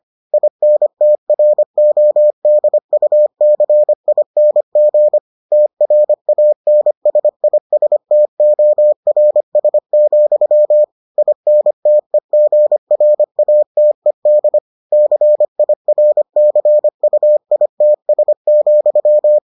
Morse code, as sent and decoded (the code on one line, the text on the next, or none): .. -. - .-. --- -.. ..- -.-. .. -. --. / - .-. .- -. ... .. ... - --- .-. ... --..-- / .. -. - . --. .-. .- - . -.. / -.-. .. .-. -.-. ..- .. - ... --..--
INTRODUCING TRANSISTORS, INTEGRATED CIRCUITS,